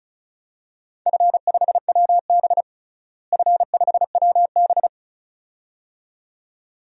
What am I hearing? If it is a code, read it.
Morse code, heard as a keyed tone, about 36 wpm, 700 Hz. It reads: F5WB F5WB
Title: DXCC-100 Call Signs - 4 Characters - Repeated 2x - 35wpm